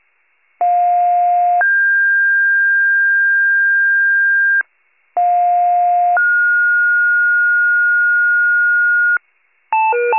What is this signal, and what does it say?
Two-tone page, heard as a one-second tone, then a three-second tone, about 710 then 1600 Hz, about 710 then 1430 Hz, with noise. It is heard from 0.6 to 9.2 s.
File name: wcalert